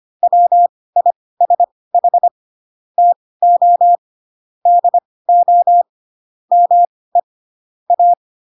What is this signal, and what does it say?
Morse code, heard as a keyed tone, about 25 wpm, spaced as for 18 wpm, 700 Hz. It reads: WISH TO DO ME A
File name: Pinocchio / Pinocchio ch4 12 25 ik1hge